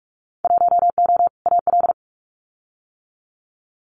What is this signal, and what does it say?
Morse code, heard as a keyed tone, about 46 wpm, 700 Hz. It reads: 1O AL
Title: Winter Field Day Exchanges Encore 45wpm